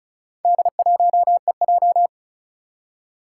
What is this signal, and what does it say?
Morse code, heard as a keyed tone, about 36 wpm, 700 Hz. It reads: D1EJ
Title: Four Challenging Letter-Number Combinations 35wpm